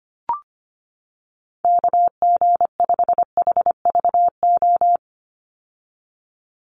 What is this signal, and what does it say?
Morse code, heard as a keyed tone, about 25 wpm, 700 Hz. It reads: KG5HVO
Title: Call Signs - 25wpm